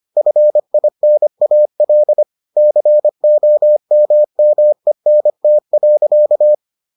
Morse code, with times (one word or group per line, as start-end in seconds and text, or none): 0.2-2.2 FINAL
2.6-6.5 COMMENT.